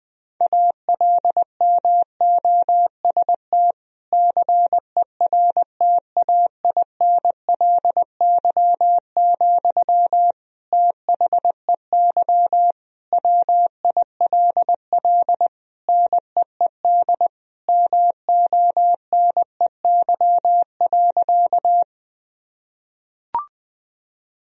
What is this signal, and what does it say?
Morse code, heard as a keyed tone, about 20 wpm, 700 Hz. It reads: ALMOST CERTAINLY, THEY WILL NEED MONEY.
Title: Sentences from Top 1000 Words - Review Part 4 - 20wpm